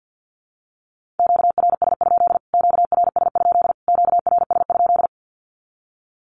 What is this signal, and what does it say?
Morse code, heard as a keyed tone, about 50 wpm, 700 Hz. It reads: QRS? QRS? QRS?
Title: Q-Codes Repeated 3x 50wpm